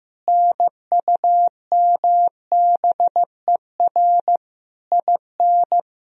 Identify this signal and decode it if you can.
Morse code, heard as a keyed tone, about 15 wpm, 700 Hz. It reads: NUMBER IN
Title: Sentences from Top 1000 Words - Review Part 1 - 15wpm